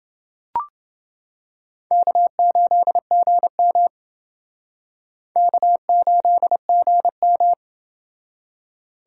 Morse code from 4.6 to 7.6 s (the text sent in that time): K8GM